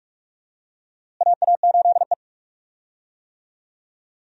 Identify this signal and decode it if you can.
Morse code, heard as a keyed tone, about 47 wpm, 700 Hz. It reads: AA8E
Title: USA Call Signs - 4 Characters - Repeated 2x - 45wpm